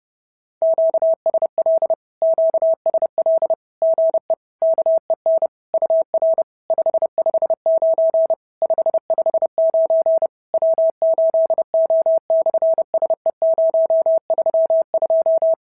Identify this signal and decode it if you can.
Morse code, heard as a keyed tone, about 28 wpm, 650 Hz. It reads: QSL QSL GE KEN UR 559 559 W8O/SE032
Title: SOTA QSO Practice 30wpm